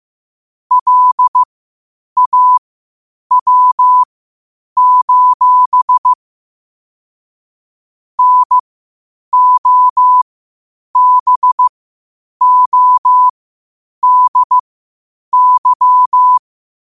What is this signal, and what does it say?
Morse code, heard as a keyed tone, about 15 wpm, spaced as for 8 wpm, 1000 Hz. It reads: LAW: NOBODY
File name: qotdfiles / 8wpm